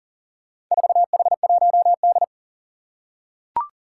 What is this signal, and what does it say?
Morse code, heard as a keyed tone, about 38 wpm, 700 Hz. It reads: 4H1D
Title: Four Challenging Letter-Number Combinations 40wpm